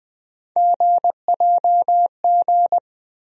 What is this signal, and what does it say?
Morse code, heard as a keyed tone, about 20 wpm, 700 Hz. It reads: GJG
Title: USA Call Sign Postfixes - 3 Letters 20wpm